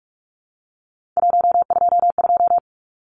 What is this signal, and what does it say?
Morse code, heard as a keyed tone, about 44 wpm, 700 Hz. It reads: 122
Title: Numbers - 3-Digit 45wpm